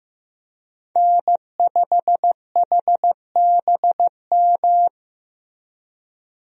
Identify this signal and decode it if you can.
Morse code, heard as a keyed tone, about 16 wpm, 700 Hz. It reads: N5HBM
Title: USA Call Signs - 5 Characters - Repeated 3x - 15wpm